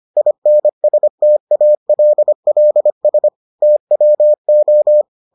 Morse code, heard as a keyed tone, about 25 wpm, 600 Hz. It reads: INSTALLS TWO